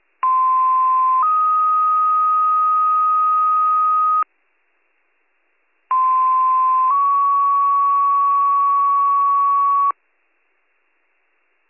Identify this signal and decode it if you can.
Two-tone page, heard as a one-second tone, then a three-second tone, about 1040 then 1250 Hz, about 1040 then 1120 Hz. It is heard from 0.2 to 9.9 s.